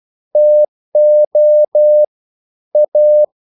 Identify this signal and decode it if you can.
Morse code, heard as a keyed tone, about 12 wpm, 600 Hz. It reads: TO A
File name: CW / Practice / EnglishQuotations0112